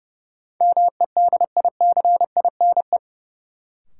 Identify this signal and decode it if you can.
Morse code, heard as a keyed tone, about 30 wpm, 700 Hz. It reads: MEDICINE
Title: Sentences from Top 900 Words - New Word Drill - 30wpm